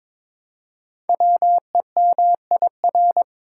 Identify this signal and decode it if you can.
Morse code, heard as a keyed tone, about 23 wpm, 700 Hz. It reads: WEMIR
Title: Code Groups - 5 Letters - 22wpm